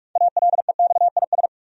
Morse code, heard as a keyed tone, about 46 wpm, 700 Hz. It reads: ALEXIS
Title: USA Common Names - 45wpm